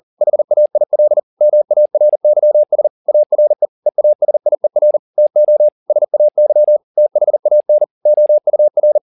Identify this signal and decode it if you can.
Morse code, heard as a keyed tone, about 40 wpm, 600 Hz. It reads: HAIL MARYS ARE EASIER TO SAY THAN OUR